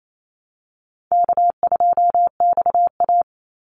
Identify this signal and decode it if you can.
Morse code, heard as a keyed tone, about 28 wpm, 700 Hz. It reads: K2XA
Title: Easy Call Signs Repeated 2x - 28wpm